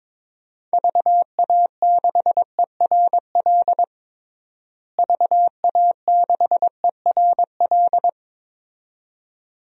Morse code, heard as a keyed tone, about 22 wpm, 700 Hz. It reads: VA6ERL VA6ERL
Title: DXCC-100 Call Signs - 6 Characters - Repeated 2x - 22wpm